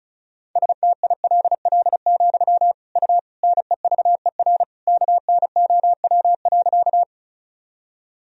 Morse code, heard as a keyed tone, about 34 wpm, 700 Hz. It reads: STILL, U NEVER KNOW.
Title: Sentences from Top 300 Words and Abbreviations 35wpm